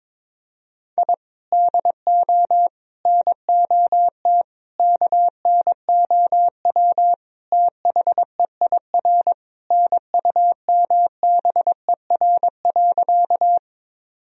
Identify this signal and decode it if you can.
Morse code, heard as a keyed tone, about 22 wpm, 700 Hz. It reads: I DO NOT KNOW THEIR NUMBER.
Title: Sentences from Top 1000 Words - Review Part 1 - 22wpm